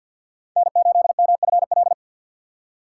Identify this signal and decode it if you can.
Morse code, heard as a keyed tone, about 53 wpm, 700 Hz. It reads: N8MFL
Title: USA Call Signs - 3x Character Spacing - 50wpm